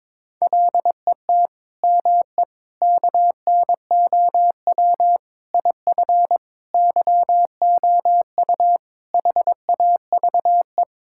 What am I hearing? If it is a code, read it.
Morse code, heard as a keyed tone, about 22 wpm, 700 Hz. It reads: LET ME KNOW IF YOU HAVE
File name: practice / 22wpm / Sentences with 8-Word Phrases - 22wpm